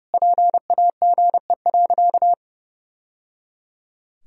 Morse code, heard as a keyed tone, about 30 wpm, 700 Hz. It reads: PAGE.
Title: Sentences from Top 1000 Words - Review Part 2 - 30wpm